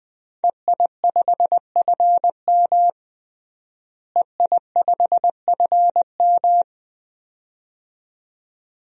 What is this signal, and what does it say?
Morse code, heard as a keyed tone, about 20 wpm, 700 Hz. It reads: EI5FM EI5FM
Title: DXCC-100 Call Signs - 5 Characters - Repeated 2x - 20wpm